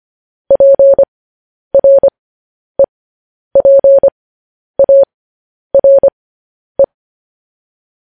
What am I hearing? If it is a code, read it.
Morse code, heard as a keyed tone, about 25 wpm, spaced as for 10 wpm, 550 Hz. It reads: PREPARE
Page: news-headlines-in-morse-code-at-10-wpm